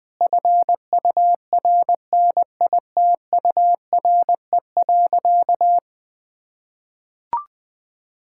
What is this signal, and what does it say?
Morse code, heard as a keyed tone, about 20 wpm, 700 Hz. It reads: FURNITURE.
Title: Sentences with 4-Word Phrases - 20wpm